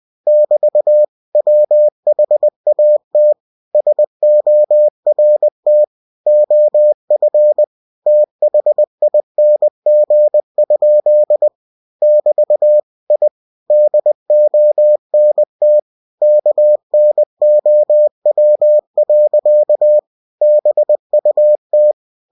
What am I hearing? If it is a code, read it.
Morse code, heard as a keyed tone, about 20 wpm, 600 Hz. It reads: <BT> WHAT SORT OF THING? <BT> I DONT KNOW. BUT
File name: cw-books / winnie-the-pooh / winnie-20wpm-page-0007